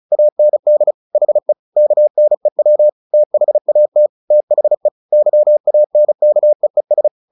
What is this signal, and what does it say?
Morse code, heard as a keyed tone, about 34 wpm, 600 Hz. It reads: AND HE KNEW THAT THE YANKEES
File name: the old man and the sea / the old man and the sea 1 35wpm